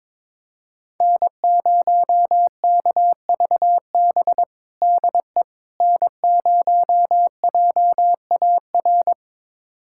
Morse code, heard as a keyed tone, about 22 wpm, 700 Hz. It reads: N0KVB DE N0JAR
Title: USA Callsign DE Callsign - 5 Characters - 22wpm